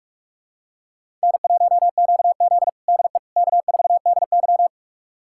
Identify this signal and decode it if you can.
Morse code, heard as a keyed tone, about 46 wpm, 700 Hz. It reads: N1QZ DE K4DY
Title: USA Callsign DE Callsign - 4 Characters - 45wpm